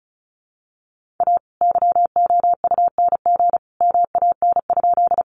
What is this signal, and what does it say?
Morse code, heard as a keyed tone, about 34 wpm, 700 Hz. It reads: A YOUNG MAN?
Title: Sentences from Top 400 Words 35wpm